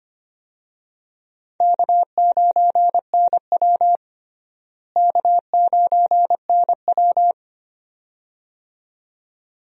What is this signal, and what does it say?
Morse code, heard as a keyed tone, about 25 wpm, 700 Hz. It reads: K9NW K9NW
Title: USA Call Signs - 4 Characters - Repeated 2x - 25wpm